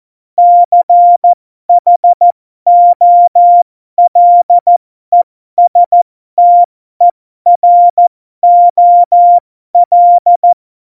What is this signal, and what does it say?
Morse code, heard as a keyed tone, about 14 wpm, spaced as for 12 wpm, 700 Hz. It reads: CHOLESTEROL
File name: Morsekurs / Witze